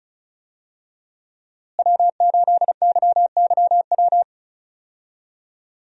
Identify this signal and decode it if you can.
Morse code, heard as a keyed tone, about 34 wpm, 700 Hz. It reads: W8YYW
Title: USA Call Signs - 5 Characters - Repeated 3x - 35wpm